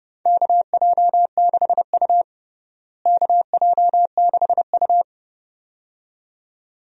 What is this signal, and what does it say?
Morse code, heard as a keyed tone, about 29 wpm, 700 Hz. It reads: KJ6U KJ6U
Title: USA Call Signs - 4 Characters - Repeated 2x - 30wpm